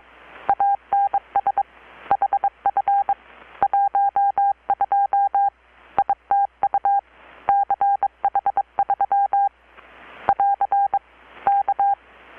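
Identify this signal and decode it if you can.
Morse code, heard as a keyed tone, about 22 wpm, 800 Hz. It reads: ANS HF 12 ITU CH3 <AR> K
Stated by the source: Morse from KPH's Night of Nights XX maritime radiotelegraphy event, received in Chiapas, Mexico about 3,800 km away on a Kenwood R-600 with a grounded 10 m long-wire antenna.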